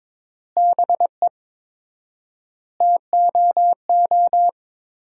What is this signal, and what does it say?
Morse code, heard as a keyed tone, about 22 wpm, 700 Hz. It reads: BE TOO
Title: 4x Word-Spacing - Sentences from Top 300 Words - 22wpm